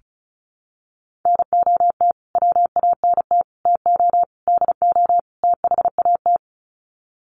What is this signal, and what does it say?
Morse code, heard as a keyed tone, about 34 wpm, 700 Hz. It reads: NOT WANT TO DO THAT
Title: Sets of 5 Words Encore 35wpm